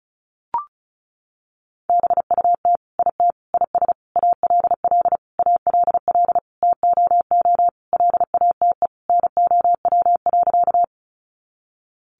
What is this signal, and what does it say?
Morse code, heard as a keyed tone, about 34 wpm, 700 Hz. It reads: BUT IT IS ALL ALL TOO LATE NOW.
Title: Sentences from Top 300 Words 35wpm